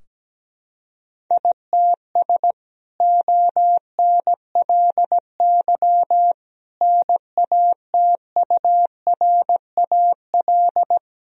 Morse code, heard as a keyed tone, about 17 wpm, 700 Hz. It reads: ITS ONLY NATURAL
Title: Sentences from Top 1000 Words - Review Part 4 - 17wpm